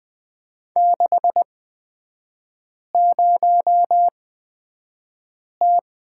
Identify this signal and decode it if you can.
Morse code, heard as a keyed tone, about 20 wpm, spaced as for 5 wpm, 700 Hz. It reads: 60T
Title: 236 Lesson 38e - Letter q - Three Characters 20wpm